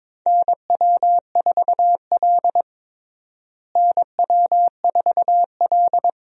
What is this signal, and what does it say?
Morse code, heard as a keyed tone, about 21 wpm, 700 Hz. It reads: NW4L NW4L
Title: USA Call Signs - 4 Characters - Repeated 3x - 22wpm